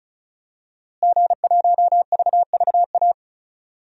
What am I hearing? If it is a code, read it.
Morse code, heard as a keyed tone, about 34 wpm, 700 Hz. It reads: G1VVA
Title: DXCC-100 Call Signs - 5 Characters - Repeated 2x - 35wpm